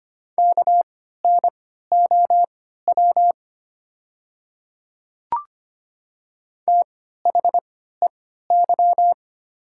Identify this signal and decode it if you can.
Morse code, heard as a keyed tone, about 25 wpm, spaced as for 14 wpm, 700 Hz. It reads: KNOW THEY
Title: 3x Character Spacing - Top 100 Words - 25wpm